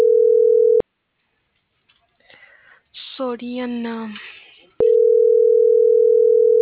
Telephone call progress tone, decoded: ringback tone

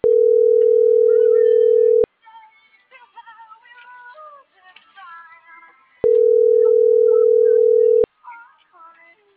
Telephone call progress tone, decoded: ringback tone

